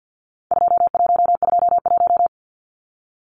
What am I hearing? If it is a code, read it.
Morse code, heard as a keyed tone, about 50 wpm, 700 Hz. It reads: 2121
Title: Numbers - 4-Digit 50wpm